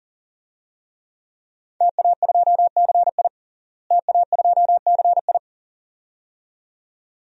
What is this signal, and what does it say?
Morse code, heard as a keyed tone, about 40 wpm, 700 Hz. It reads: TA2CI TA2CI
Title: DXCC-100 Call Signs - 5 Characters - Repeated 2x - 40wpm